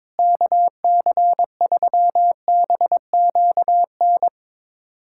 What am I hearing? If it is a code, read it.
Morse code, heard as a keyed tone, about 22 wpm, 700 Hz. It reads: KC3BQN
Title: USA Callsign DE Callsign - 6 Characters - 22wpm